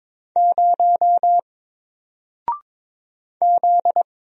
Morse code, heard as a keyed tone, about 22 wpm, 700 Hz. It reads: 0 Z
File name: Single-Letter-Number-22wpm